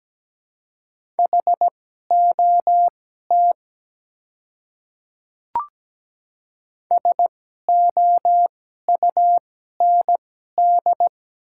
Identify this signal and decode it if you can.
Morse code, heard as a keyed tone, about 17 wpm, spaced as for 12 wpm, 690 Hz. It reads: HOT SOUND
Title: 2x Character Spacing - Top 100 Words - 17wpm